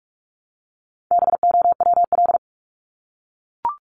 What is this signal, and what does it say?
Morse code, heard as a keyed tone, about 44 wpm, 700 Hz. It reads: BOWL
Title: Top 1001-1250 Words 45wpm